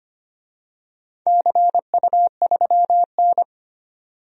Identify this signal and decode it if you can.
Morse code, heard as a keyed tone, about 25 wpm, 700 Hz. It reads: CU3N
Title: DXCC-100 Call Signs - 4 Characters - 25wpm